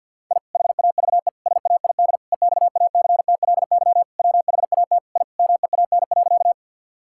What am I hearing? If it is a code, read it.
Morse code, heard as a keyed tone, about 50 wpm, 700 Hz. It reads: I HAVE SAID EXACTLY WHAT I MEAN.
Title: Sentences from Top 1000 Words - Review Part 6 - 50wpm